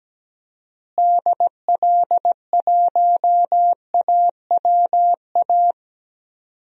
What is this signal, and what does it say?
Morse code, heard as a keyed tone, about 17 wpm, 700 Hz. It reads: DL1AWA
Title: DXCC-100 Call Signs - 6 Characters - Repeated 3x - 17wpm